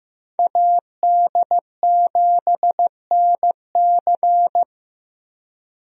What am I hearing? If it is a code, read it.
Morse code, heard as a keyed tone, about 15 wpm, 700 Hz. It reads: AD7NC
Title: USA Call Signs - 5 Characters - Repeated 3x - 15wpm